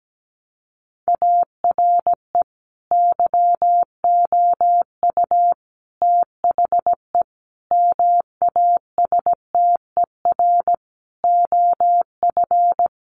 Morse code, heard as a keyed tone, about 17 wpm, 700 Hz. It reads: ARE YOU THE MASTER OF
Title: Sentences from Top 1300 Words 17wpm